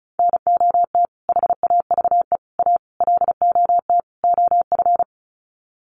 Morse code, heard as a keyed tone, about 34 wpm, 700 Hz. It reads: NOT HAVE A LOT OF
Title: Sets of 5 Words 35wpm